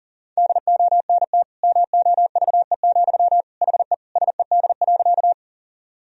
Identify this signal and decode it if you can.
Morse code, heard as a keyed tone, about 40 wpm, 700 Hz. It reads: DONT MOVE, HE SED.
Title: Sentences from Top 300 Words and Abbreviations 40wpm